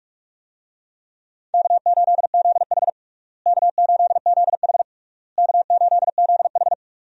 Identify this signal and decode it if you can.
Morse code, heard as a keyed tone, about 46 wpm, 700 Hz. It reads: K8ZH K8ZH K8ZH
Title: USA Call Signs - 4 Characters - Repeated 3x - 45wpm